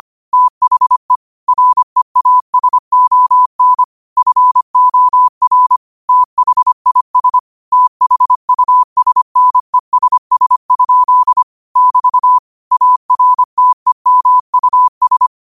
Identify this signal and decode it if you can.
Morse code, heard as a keyed tone, about 25 wpm, 1000 Hz. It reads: THE REASON FOR THIS THUSNESS? <BT> ARTEMUS
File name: qotdfiles / 25wpm